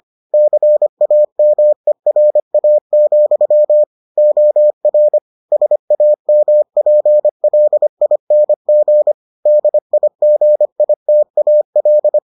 Morse code, heard as a keyed tone, about 25 wpm, 600 Hz. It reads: CAMERA, OR SAMPLING DIGITAL